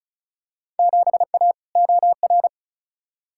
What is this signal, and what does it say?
Morse code, heard as a keyed tone, about 36 wpm, 690 Hz. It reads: 7A OR
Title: Field Day Exchanges Encore - 35wpm